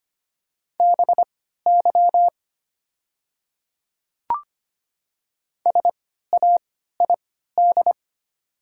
Morse code, heard as a keyed tone, about 25 wpm, spaced as for 14 wpm, 700 Hz. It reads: BY SAID
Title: 3x Character Spacing - Top 100 Words - 25wpm